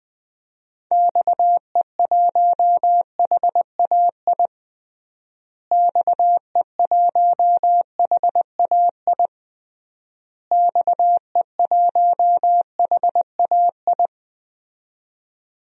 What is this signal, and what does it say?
Morse code, heard as a keyed tone, about 20 wpm, 700 Hz. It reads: XE1HAI XE1HAI XE1HAI
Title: DXCC-100 Call Signs - 6 Characters - Repeated 3x - 20wpm